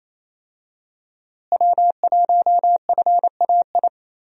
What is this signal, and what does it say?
Morse code, heard as a keyed tone, about 28 wpm, 700 Hz. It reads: W1FAI
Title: USA Call Signs - 5 Characters - Repeated 2x - 28wpm